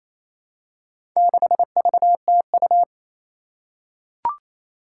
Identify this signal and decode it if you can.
Morse code, heard as a keyed tone, about 28 wpm, 700 Hz. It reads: 6VTU
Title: Four Challenging Letter-Number Combinations 28wpm